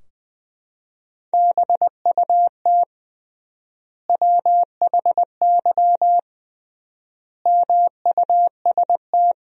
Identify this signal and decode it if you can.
Morse code, heard as a keyed tone, about 21 wpm, 700 Hz. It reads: BUT WHY MUST
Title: 3x Word-Spacing - Sentences from Top 400 Words - 20wpm